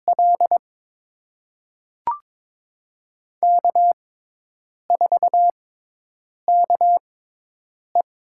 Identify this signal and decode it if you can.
Morse code, heard as a keyed tone, about 22 wpm, spaced as for 8 wpm, 700 Hz. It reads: L K4KE
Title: USA Call Signs - 6x Character Spacing - 22wpm